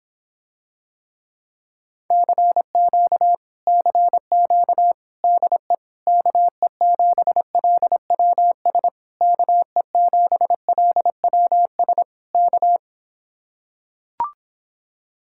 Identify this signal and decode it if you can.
Morse code, heard as a keyed tone, about 26 wpm, 700 Hz. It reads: CQ CQ DE KE7LWS KE7LWS K